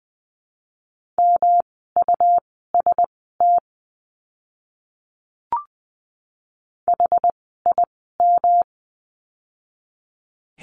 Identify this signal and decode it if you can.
Morse code, heard as a keyed tone, about 20 wpm, spaced as for 14 wpm, 700 Hz. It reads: MUST HIM
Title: 237 Lesson 38f - Letter q - Words 20wpm